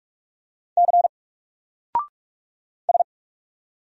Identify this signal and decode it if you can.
Morse code, heard as a keyed tone, about 46 wpm, 700 Hz. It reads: C S
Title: Single Letter-Number 45wpm